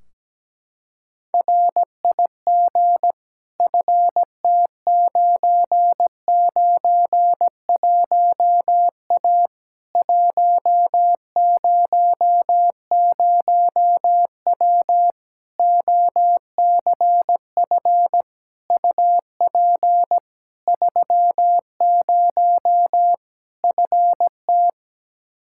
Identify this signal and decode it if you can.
Morse code, heard as a keyed tone, about 17 wpm, 700 Hz. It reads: RIG FT991A 100W OCF UP 30 FT
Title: Practice QSO - Exchange 2 - Info - 17wpm